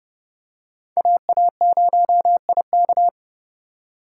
Morse code, heard as a keyed tone, about 30 wpm, 700 Hz. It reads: AA0IK